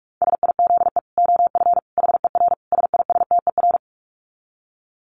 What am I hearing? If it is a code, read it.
Morse code, heard as a keyed tone, about 44 wpm, 700 Hz. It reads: SIZE OF HER SISTER
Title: Dit Challenge 45wpm